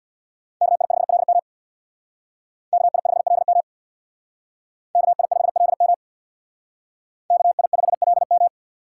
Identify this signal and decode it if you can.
Morse code, heard as a keyed tone, about 62 wpm, 700 Hz. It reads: KI5LM KI5LM KI5LM KI5LM
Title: Speed Racing - USA 2x2 Call Signs 50wpm